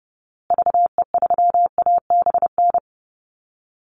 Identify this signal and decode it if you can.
Morse code, heard as a keyed tone, about 30 wpm, 700 Hz. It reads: VE3ABN
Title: Call Signs Repeated 2x - 30wpm